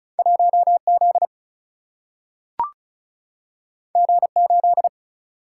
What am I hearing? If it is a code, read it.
Morse code, heard as a keyed tone, about 36 wpm, 700 Hz. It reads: 1Z G8
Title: Single-Letter Single-Number 35wpm